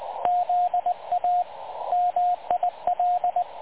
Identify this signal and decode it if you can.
Morse code, heard as a keyed tone, about 22 wpm, 750 Hz, with radio noise. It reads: ZA MIL
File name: dlazaawansowanych / nagraniazpasma / websdr recording 3565.0kHz